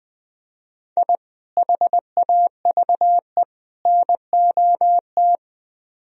Morse code, heard as a keyed tone, about 20 wpm, 700 Hz. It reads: I HAVE NOT
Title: Speed-Racing - Sets of 3 Words 17wpm